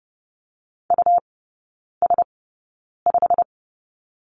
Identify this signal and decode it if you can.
Morse code, heard as a keyed tone, about 30 wpm, spaced as for 9 wpm, 700 Hz. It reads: US5
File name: Lesson 19e - Letter w - Three Characters 30wpm